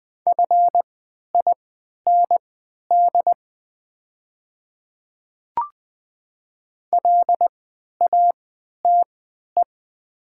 Morse code, heard as a keyed tone, about 20 wpm, spaced as for 11 wpm, 700 Hz. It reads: FIND LATE
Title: 3x Character Spacing - Top 300 Words - 20wpm